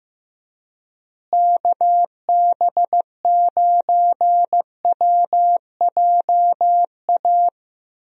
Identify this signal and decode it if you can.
Morse code, heard as a keyed tone, about 15 wpm, 700 Hz. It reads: KB9WJA
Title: USA Call Signs - 6 Characters - Repeated 3x - 15wpm